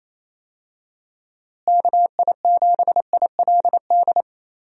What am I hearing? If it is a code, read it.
Morse code, heard as a keyed tone, about 28 wpm, 700 Hz. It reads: KI7ILD
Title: USA Call Signs - 6 Characters - Repeated 3x - 28wpm